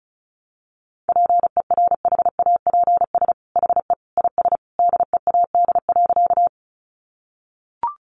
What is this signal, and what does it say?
Morse code, heard as a keyed tone, about 34 wpm, 700 Hz. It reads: PERHAPS HE IS DEAD.
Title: Sentences from Top 1500 Words 35wpm